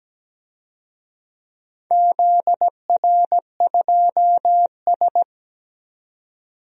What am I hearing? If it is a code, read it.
Morse code, heard as a keyed tone, about 17 wpm, 700 Hz. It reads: ZR2S
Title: DXCC-100 Call Signs - 4 Characters - 17wpm